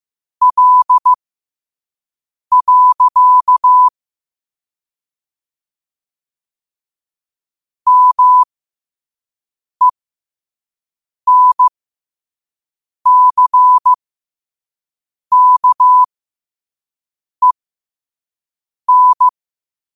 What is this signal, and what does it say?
Morse code, heard as a keyed tone, about 15 wpm, spaced as for 5 wpm, 1000 Hz. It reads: L. MENCKEN
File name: qotdfiles / 5wpm